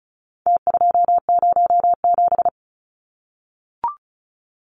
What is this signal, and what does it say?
Morse code, heard as a keyed tone, about 34 wpm, 700 Hz. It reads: T207
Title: Numbers - Contest Style - 4-Digit 35wpm